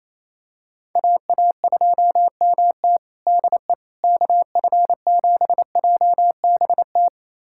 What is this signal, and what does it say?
Morse code, heard as a keyed tone, about 28 wpm, 700 Hz. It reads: AA2MT DE KF7JBT
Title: USA Callsign DE Callsign 28wpm